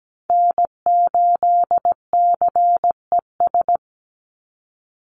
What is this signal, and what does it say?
Morse code, heard as a keyed tone, about 17 wpm, 700 Hz. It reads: N8CES
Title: USA Call Signs 17wpm